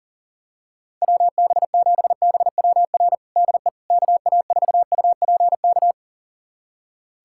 Morse code, heard as a keyed tone, about 40 wpm, 700 Hz. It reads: WB7BWR DE KA4UPK
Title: USA Callsign DE Callsign 40wpm